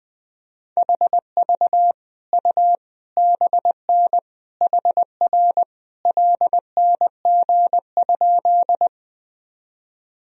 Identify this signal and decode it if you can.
Morse code, heard as a keyed tone, about 20 wpm, 700 Hz. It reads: HV U BN HR LNG?
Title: Sentences from Top 200 Words and Abbreviations 20wpm